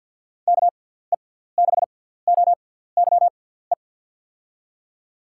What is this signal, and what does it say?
Morse code, heard as a keyed tone, about 47 wpm, spaced as for 17 wpm, 700 Hz. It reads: KE6OYE
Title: USA Call Signs - 6 Characters - 6x Character Spacing - 50wpm